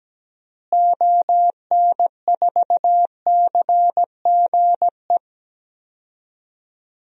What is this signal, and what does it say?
Morse code, heard as a keyed tone, about 17 wpm, 700 Hz. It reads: ON4CGE
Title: DXCC-100 Call Signs - 6 Characters - Repeated 2x - 17wpm